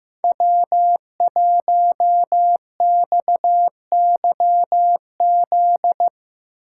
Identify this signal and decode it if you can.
Morse code, heard as a keyed tone, about 15 wpm, 700 Hz. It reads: W1XYZ
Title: USA Call Signs - 5 Characters - 4x Character Spacing - 15wpm